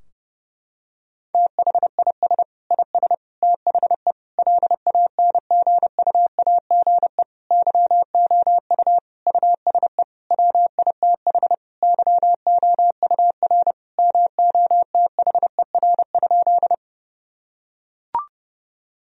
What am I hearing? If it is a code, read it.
Morse code, heard as a keyed tone, about 30 wpm, 700 Hz. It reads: THIS IS THE LANGUAGE YOU USE WITH YOUR MOTHER?
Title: Sentences from Top 1000 Words - Review Part 3 - 30wpm